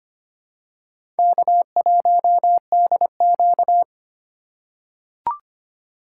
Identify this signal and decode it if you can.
Morse code, heard as a keyed tone, about 25 wpm, 700 Hz. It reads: K1DQ
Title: USA Call Signs - 4 Characters - 2x Character Spacing - 25wpm